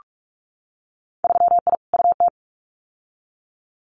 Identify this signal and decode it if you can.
Morse code, heard as a keyed tone, about 46 wpm, 700 Hz. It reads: 3I UT
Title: Winter Field Day Exchanges Encore 45wpm